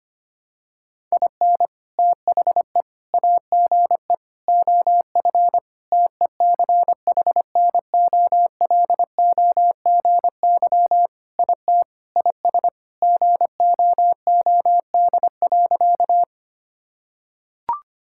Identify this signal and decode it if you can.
Morse code, heard as a keyed tone, about 25 wpm, 700 Hz. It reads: IN THE AGE OF TECHNOLOGY IT IS GOOD.
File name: practice / 25wpm / Sentences with 3-Word Phrases - 25wpm